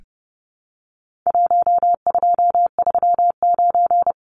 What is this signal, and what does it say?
Morse code, heard as a keyed tone, about 30 wpm, 700 Hz. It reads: 1239
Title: Numbers - 4-Digit 30wpm